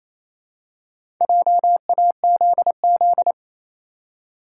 Morse code, heard as a keyed tone, about 27 wpm, 700 Hz. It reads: JAZZ
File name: Easy Words - Speed-Racing with Hints - 22wpm